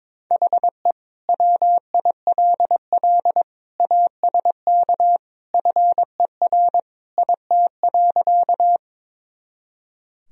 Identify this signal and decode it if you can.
Morse code, heard as a keyed tone, about 22 wpm, 700 Hz. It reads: HE WILL ASK FER IT.
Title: Sentences from Top 200 Words and Abbreviations 22wpm